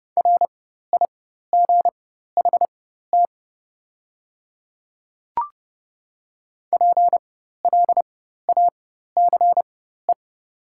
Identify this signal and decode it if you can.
Morse code, heard as a keyed tone, about 30 wpm, spaced as for 14 wpm, 700 Hz. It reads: RIGHT PLACE
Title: 4x Character Spacing - Top 200 Words - 30wpm